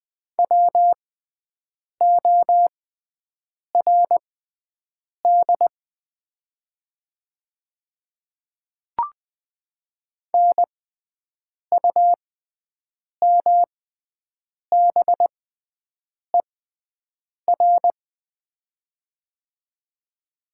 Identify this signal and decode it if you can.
Morse code, heard as a keyed tone, about 20 wpm, spaced as for 7 wpm, 700 Hz. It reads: WORD NUMBER
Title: 6x Character Spacing - Top 100 Words - 20wpm